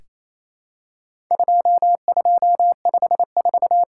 Morse code, heard as a keyed tone, about 28 wpm, 700 Hz. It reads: 2254
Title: Worked All Europe DX Contest - QTC Exchange - 28wpm